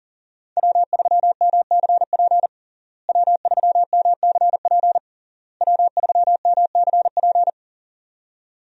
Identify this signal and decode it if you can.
Morse code, heard as a keyed tone, about 40 wpm, 700 Hz. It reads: W3MCP W3MCP W3MCP
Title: USA Call Signs - 5 Characters - Repeated 3x - 40wpm